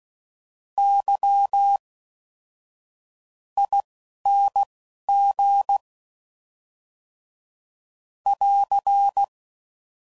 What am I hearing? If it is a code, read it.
Morse code, heard as a keyed tone, about 16 wpm, spaced as for 7 wpm, 800 Hz. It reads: YING <AR>